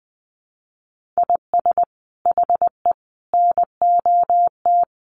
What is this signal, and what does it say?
Morse code, heard as a keyed tone, about 20 wpm, 700 Hz. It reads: IS HE NOT
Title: Sentences from Top 1500 Words 20wpm